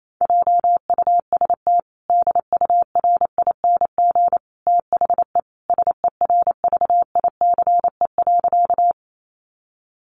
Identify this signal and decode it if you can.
Morse code, heard as a keyed tone, about 28 wpm, 700 Hz. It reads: JUST DURING THE SERVICE.
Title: Sentences from Top 700 Words 28wpm